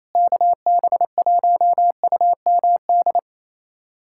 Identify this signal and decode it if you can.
Morse code, heard as a keyed tone, about 28 wpm, 700 Hz. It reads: KB1UMD
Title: USA Callsign DE Callsign - 6 Characters - 28wpm